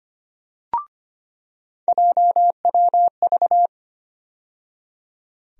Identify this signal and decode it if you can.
Morse code, heard as a keyed tone, about 25 wpm, 700 Hz. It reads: JWV
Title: USA Call Sign Postfixes - 3 Letters 25wpm